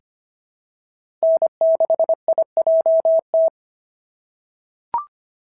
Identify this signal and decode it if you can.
Morse code, heard as a keyed tone, about 25 wpm, 690 Hz. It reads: N6IJT
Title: SOTA QSO Practice 25wpm